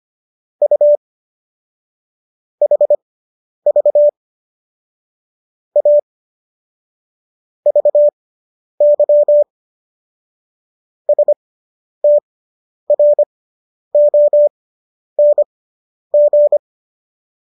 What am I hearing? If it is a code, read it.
Morse code, heard as a keyed tone, about 25 wpm, spaced as for 10 wpm, 590 Hz. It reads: U HV A VY STRONG